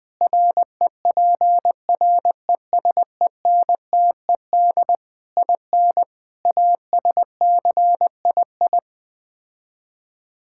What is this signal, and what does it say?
Morse code, heard as a keyed tone, about 20 wpm, 700 Hz. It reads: REPRESENTED IN ASCII
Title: Build A Phrase - 5-Word Phrases - 20wpm